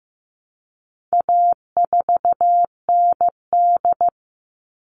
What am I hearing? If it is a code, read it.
Morse code, heard as a keyed tone, about 15 wpm, 700 Hz. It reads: A4ND
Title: All Cut Numbers - 4-Digit 15wpm